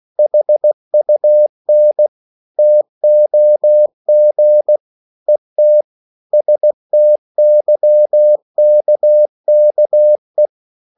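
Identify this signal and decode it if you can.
Morse code, heard as a keyed tone, about 16 wpm, 600 Hz. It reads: HUN TOG ET STYKKE
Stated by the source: Slow (QRS) Morse code at 16 wpm, from CW med Gnister.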